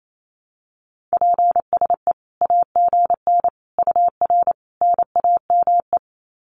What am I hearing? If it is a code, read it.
Morse code, heard as a keyed tone, about 28 wpm, 700 Hz. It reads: PSE AGN UR NAME
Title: Common QSO Phrases 28wpm